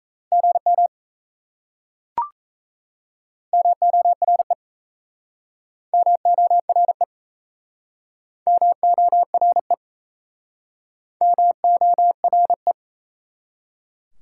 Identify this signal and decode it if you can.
Morse code, heard as a keyed tone, about 37 wpm, 700 Hz. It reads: GM MORE MORE MORE MORE